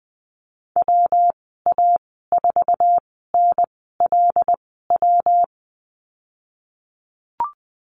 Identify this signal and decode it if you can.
Morse code, heard as a keyed tone, about 20 wpm, spaced as for 15 wpm, 700 Hz. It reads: WA4NLW
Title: 167 Lesson 27g - Number 7 - Call Signs 20wpm